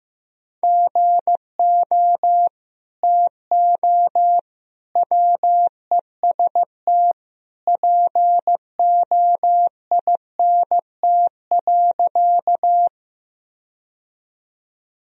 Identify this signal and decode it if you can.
Morse code, heard as a keyed tone, about 15 wpm, 700 Hz. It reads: GO TO WEST POINT.
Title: Sentences from Top 1000 Words - Review Part 2 - 15wpm